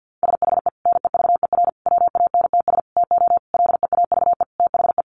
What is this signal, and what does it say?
Morse code, heard as a keyed tone, about 50 wpm, 700 Hz. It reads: SHE NEVER WANTS TO LEAVE THE
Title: Easy Phrases 50wpm